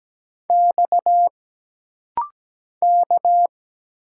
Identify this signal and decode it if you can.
Morse code, heard as a keyed tone, about 17 wpm, 700 Hz. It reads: X K
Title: Single Letters - Rapid-Fire 17wpm